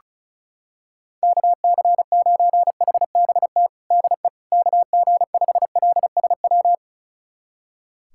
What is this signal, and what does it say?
Morse code, heard as a keyed tone, about 36 wpm, 700 Hz. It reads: KC9HBT DE KG5LSW
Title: USA Callsign DE Callsign 35wpm